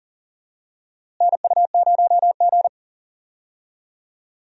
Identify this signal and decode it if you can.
Morse code, heard as a keyed tone, about 40 wpm, 700 Hz. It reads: NU0G
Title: USA Call Signs - 4 Characters - Repeated 2x - 40wpm